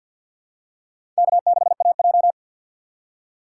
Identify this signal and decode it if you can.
Morse code, heard as a keyed tone, about 47 wpm, 700 Hz. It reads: K6AJ
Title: USA Call Signs - 4 Characters - 2x Character Spacing - 50wpm